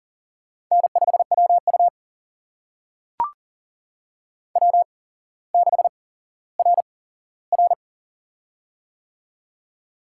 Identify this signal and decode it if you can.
Morse code, heard as a keyed tone, about 38 wpm, spaced as for 17 wpm, 700 Hz. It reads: N5WU W6RR
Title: USA Call Signs - 4 Characters - 8x Character Spacing - 40wpm